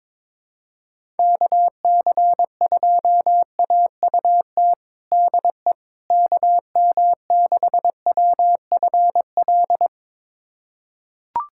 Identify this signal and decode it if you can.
Morse code, heard as a keyed tone, about 21 wpm, 700 Hz. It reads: KC2AUT DE KM6WFL
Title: USA Callsign DE Callsign - 6 Characters - 22wpm